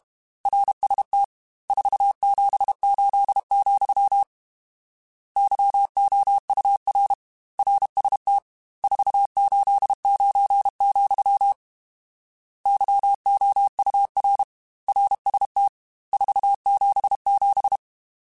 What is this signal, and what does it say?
Morse code, heard as a keyed tone, about 32 wpm, 800 Hz. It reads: RST 478, YOUR RST 489, YOUR RST 477
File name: RSTs / RST-mp3s / Your RSTs @30